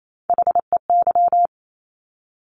Morse code, heard as a keyed tone, about 28 wpm, 700 Hz. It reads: HEY
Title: Top Three Letter Combinations in Words 28wpm